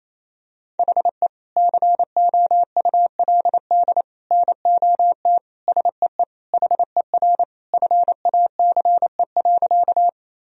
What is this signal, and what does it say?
Morse code, heard as a keyed tone, about 28 wpm, 700 Hz. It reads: HE COULD NOT SEE HER FACE.
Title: Sentences from Top 1000 Words - Review Part 2 - 28wpm